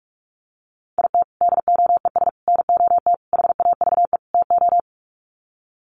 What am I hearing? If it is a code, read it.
Morse code, heard as a keyed tone, about 44 wpm, 700 Hz. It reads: IT DOES NOT HAVE TO
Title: Sets of 5 Words 45wpm